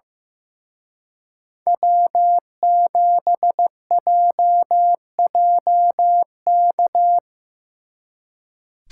Morse code, heard as a keyed tone, about 18 wpm, 700 Hz. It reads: W7JJK
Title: Speed Racing - USA 1x3 Call Signs 15wpm